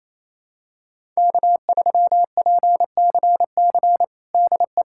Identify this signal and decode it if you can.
Morse code, heard as a keyed tone, about 28 wpm, 700 Hz. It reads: K3PCC DE
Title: USA Callsign DE Callsign - 5 Characters - 28wpm